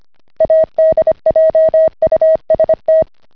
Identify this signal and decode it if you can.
Morse code, heard as a keyed tone, about 25 wpm, 640 Hz. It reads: ADJUST